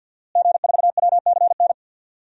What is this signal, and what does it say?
Morse code, heard as a keyed tone, about 48 wpm, 700 Hz. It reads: G4WCN
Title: DXCC-100 Call Signs - 5 Characters - Repeated 2x - 50wpm